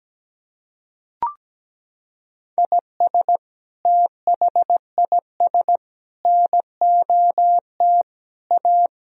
Morse code, heard as a keyed tone, about 17 wpm, 700 Hz. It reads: IS THIS NOT A
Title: Sentences from Top 1000 Words - Review Part 4 - 17wpm